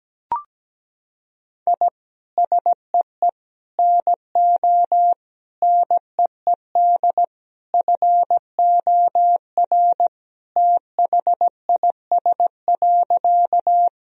Morse code, heard as a keyed tone, about 17 wpm, 700 Hz. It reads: I SEE NO NEED FOR THIS.
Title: Sentences from Top 1000 Words - Review Part 1 - 17wpm